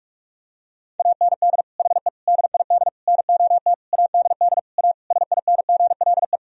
Morse code, heard as a keyed tone, about 47 wpm, 700 Hz. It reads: AND HE DID NOT ADD A SINGLE
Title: Sentences from Top 1000 Words - Review Part 1 - 45wpm